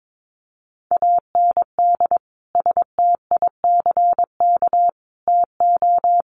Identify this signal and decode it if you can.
Morse code, heard as a keyed tone, about 22 wpm, 700 Hz. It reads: AND STICK TO
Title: Sentences from Top 1200 Words 22wpm